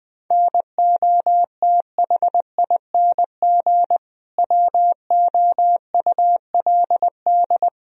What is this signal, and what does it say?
Morse code, heard as a keyed tone, about 20 wpm, 700 Hz. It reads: NOTHING WOULD
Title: Sentences from Top 1000 Words - Review Part 3 - 20wpm